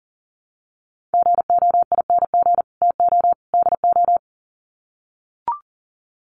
Morse code, heard as a keyed tone, about 40 wpm, 700 Hz. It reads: GOING TO DO
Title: Sets of 3 Words 40wpm